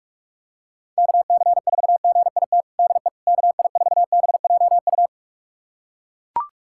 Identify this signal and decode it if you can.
Morse code, heard as a keyed tone, about 41 wpm, 700 Hz. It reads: KC4GIT DE KI4BJU
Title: USA Callsign DE Callsign 45wpm